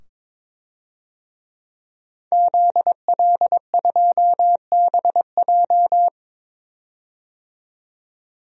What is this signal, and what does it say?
Morse code, heard as a keyed tone, about 23 wpm, 700 Hz. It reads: ZL2BJ
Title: DXCC-100 Call Signs - 5 Characters - 22wpm